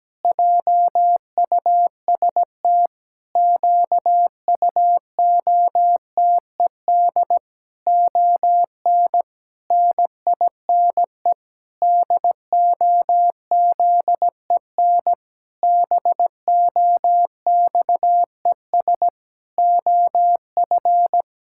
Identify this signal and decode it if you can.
Morse code, heard as a keyed tone, about 17 wpm, 700 Hz. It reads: JUST QUOTED ON NINE DOZEN BOXES OF
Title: Pangram Challenge v2 - 17wpm